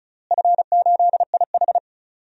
Morse code, heard as a keyed tone, about 36 wpm, 700 Hz. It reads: F8IH
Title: DXCC-100 Call Signs - 4 Characters - Repeated 2x - 35wpm